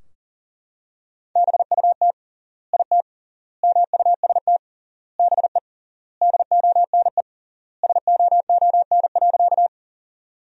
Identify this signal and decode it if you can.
Morse code, heard as a keyed tone, about 40 wpm, 700 Hz. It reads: BUT IT MUST BE DONE SOON.
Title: 3x Word-Spacing - Sentences from Top 500 Words - 40wpm